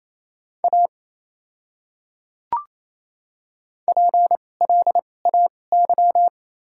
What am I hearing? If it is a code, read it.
Morse code, heard as a keyed tone, about 28 wpm, spaced as for 20 wpm, 700 Hz. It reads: A PLAY
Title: 2x Character Spacing - Top 200 Words - 28wpm